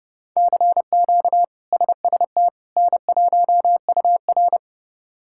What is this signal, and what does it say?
Morse code, heard as a keyed tone, about 31 wpm, 700 Hz. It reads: CQ SST N1UR
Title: K1USN SST QSO Practice - 30wpm